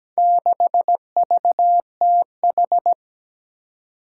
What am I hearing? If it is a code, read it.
Morse code, heard as a keyed tone, about 17 wpm, 700 Hz. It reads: 6VTH
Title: Four Challenging Letter-Number Combinations 17wpm